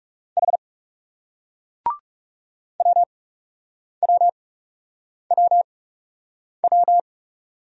Morse code, heard as a keyed tone, about 40 wpm, spaced as for 8 wpm, 700 Hz. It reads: H WWWW